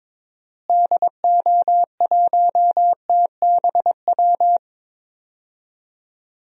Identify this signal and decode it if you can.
Morse code, heard as a keyed tone, about 22 wpm, 700 Hz. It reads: DO1TBW
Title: DXCC-100 Call Signs - 6 Characters - Repeated 2x - 22wpm